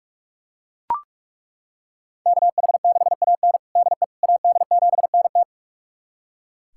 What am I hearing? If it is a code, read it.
Morse code, heard as a keyed tone, about 42 wpm, 700 Hz. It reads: KH6AN DE AD7NT
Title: USA Callsign DE Callsign 45wpm